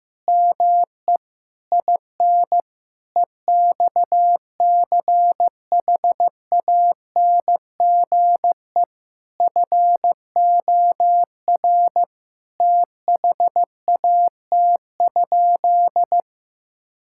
Morse code, heard as a keyed tone, about 15 wpm, 700 Hz. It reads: ME IN EXCHANGE FOR THAT?
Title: Sentences from Top 1000 Words - Review Part 6 - 15wpm